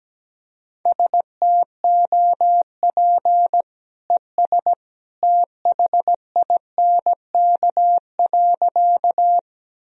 Morse code, heard as a keyed tone, about 17 wpm, 700 Hz. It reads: STOP ES THINK.
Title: Sentences from Top 300 Words and Abbreviations 17wpm